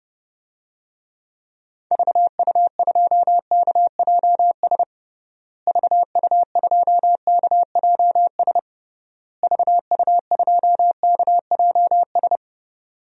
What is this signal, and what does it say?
Morse code, heard as a keyed tone, about 30 wpm, 700 Hz. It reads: VU2KJS VU2KJS VU2KJS
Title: DXCC-100 Call Signs - 6 Characters - Repeated 3x - 30wpm